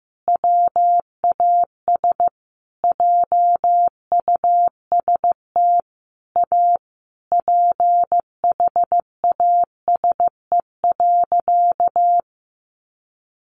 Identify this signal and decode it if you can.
Morse code, heard as a keyed tone, about 15 wpm, 700 Hz. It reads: WAS JUST A PHASE.
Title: Sentences from Top 1200 Words 15wpm